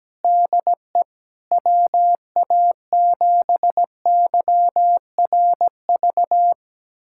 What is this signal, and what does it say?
Morse code, heard as a keyed tone, about 17 wpm, 700 Hz. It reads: DE WA7YRV
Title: USA Callsign DE Callsign 17wpm